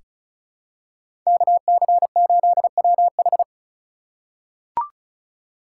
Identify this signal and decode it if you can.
Morse code, heard as a keyed tone, about 34 wpm, 700 Hz. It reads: KC8WH
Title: K1USN SST QSO Practice - 35wpm